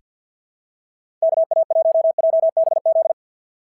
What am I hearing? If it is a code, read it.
Morse code, heard as a keyed tone, about 46 wpm, 650 Hz. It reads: KA1JBZ
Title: SOTA QSO Practice 50wpm